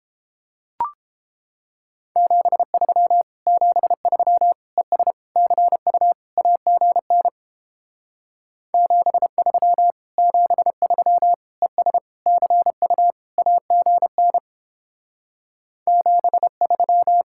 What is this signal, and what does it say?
Morse code, heard as a keyed tone, about 29 wpm, 700 Hz. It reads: 73 73 ES CU AGN 73 73 ES CU AGN 73